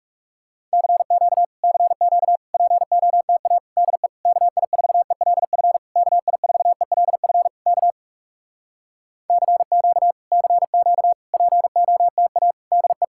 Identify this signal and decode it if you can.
Morse code, heard as a keyed tone, about 40 wpm, 700 Hz. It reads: CQ CQ POTA DE KI4ELF KI4ELF K CQ CQ POTA DE